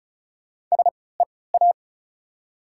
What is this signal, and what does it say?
Morse code, heard as a keyed tone, about 36 wpm, spaced as for 20 wpm, 690 Hz. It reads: SEA
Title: 3x Character Spacing - Top 300 Words - 35wpm